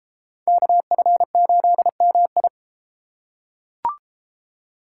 Morse code, heard as a keyed tone, about 31 wpm, 700 Hz. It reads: KF8MI